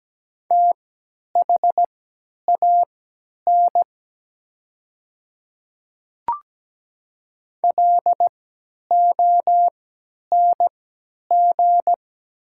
Morse code, heard as a keyed tone, about 17 wpm, spaced as for 10 wpm, 700 Hz. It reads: THAN LONG
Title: 3x Character Spacing - Top 300 Words - 17wpm